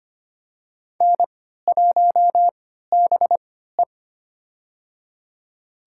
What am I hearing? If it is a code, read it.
Morse code, heard as a keyed tone, about 25 wpm, spaced as for 14 wpm, 700 Hz. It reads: N1BE
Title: USA Call Signs - 3x Character Spacing - 25wpm